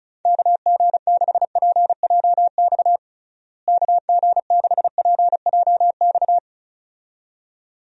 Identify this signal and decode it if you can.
Morse code, heard as a keyed tone, about 34 wpm, 700 Hz. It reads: KG6PJX KG6PJX
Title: USA Call Signs - 6 Characters - Repeated 2x - 35wpm